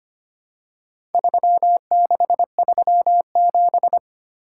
Morse code, heard as a keyed tone, about 25 wpm, 700 Hz. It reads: 3637
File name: practice / 25wpm / Four Letter-Number Combinations - 25wpm